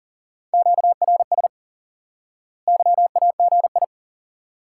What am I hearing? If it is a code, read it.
Morse code, heard as a keyed tone, about 37 wpm, 700 Hz. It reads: QRS YAGI
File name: IQE Course 3 - Lesson 89 - YAGI - Two Elements - 40wpm